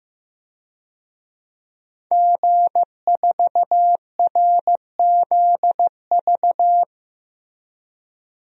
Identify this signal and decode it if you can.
Morse code, heard as a keyed tone, about 15 wpm, 700 Hz. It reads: G4RZV
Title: DXCC-100 Call Signs - 5 Characters - Repeated 3x - 15wpm